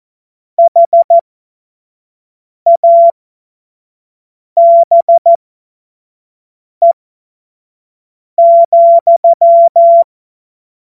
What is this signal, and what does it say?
Morse code, heard as a keyed tone, about 14 wpm, spaced as for 5 wpm, 680 Hz. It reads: HABE,
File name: Witz-23-September-2025-25-F70